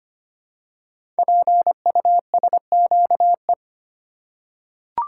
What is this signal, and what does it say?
Morse code, heard as a keyed tone, about 25 wpm, 700 Hz. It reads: PUSQE
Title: Code Groups - 5 Letters - 25wpm